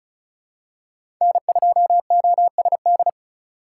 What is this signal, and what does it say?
Morse code, heard as a keyed tone, about 36 wpm, 700 Hz. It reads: N2OSD
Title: USA Call Signs - 5 Characters - 3x Character Spacing - 35wpm